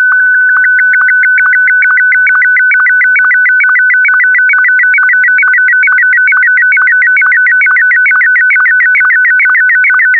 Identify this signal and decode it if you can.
SSTV (slow-scan television), Martin 1.